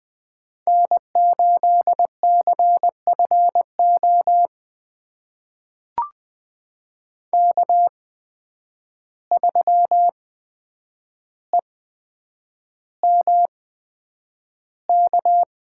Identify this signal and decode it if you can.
Morse code, heard as a keyed tone, about 20 wpm, spaced as for 9 wpm, 700 Hz. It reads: N8CFO K3EMK
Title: USA Call Signs - 5 Characters - 8x Character Spacing - 20wpm